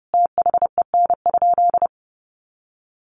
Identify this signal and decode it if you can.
Morse code, heard as a keyed tone, about 30 wpm, 700 Hz. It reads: THEN?
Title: Sentences from Top 1000 Words 30wpm